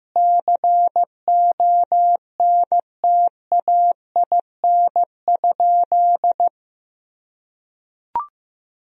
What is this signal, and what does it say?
Morse code, heard as a keyed tone, about 15 wpm, 700 Hz. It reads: CONTAIN?
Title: Sentences from Top 1000 Words - Review Part 2 - 15wpm